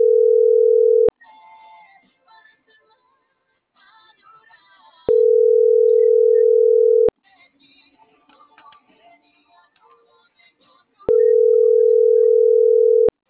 Ringback tone (North America), 440 + 480 Hz.